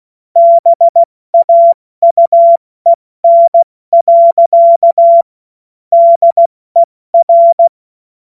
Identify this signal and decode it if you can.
Morse code, heard as a keyed tone, about 16 wpm, 680 Hz. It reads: BAUEN. DER
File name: Morsekurs / Witze / Witz 26 Dezember 2023 70 F80